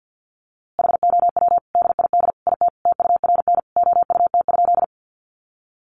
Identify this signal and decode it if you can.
Morse code, heard as a keyed tone, about 50 wpm, 700 Hz. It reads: HOW DID IT TURN OUT?
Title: A Sound Of Thunder 50wpm